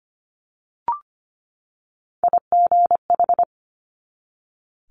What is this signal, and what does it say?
Morse code, heard as a keyed tone, about 25 wpm, 700 Hz. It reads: IGH
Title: Multi-letter Phonemes 25wpm